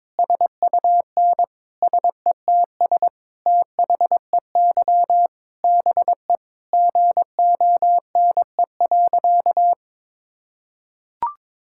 Morse code, heard as a keyed tone, about 22 wpm, 700 Hz. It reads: SUN SETS THEY BE GONE.
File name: practice / 22wpm / Sentences with 4-Word Phrases - 22wpm